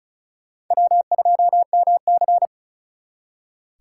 Morse code, heard as a keyed tone, about 43 wpm, 700 Hz. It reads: W2MC
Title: Speed Racing - USA 1x2 Call Signs 35wpm